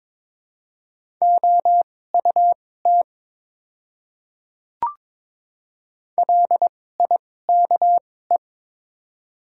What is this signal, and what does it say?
Morse code, heard as a keyed tone, about 22 wpm, spaced as for 16 wpm, 700 Hz. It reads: OUT LIKE
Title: 2x Character Spacing - Top 200 Words - 22wpm